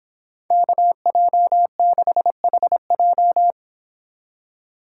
Morse code, heard as a keyed tone, about 26 wpm, 700 Hz. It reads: KJ6HJ